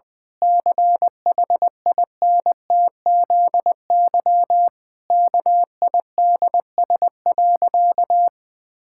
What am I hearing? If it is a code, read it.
Morse code, heard as a keyed tone, about 20 wpm, 700 Hz. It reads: CHINTZY KIDS.
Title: Pangram Challenge v2 - 20wpm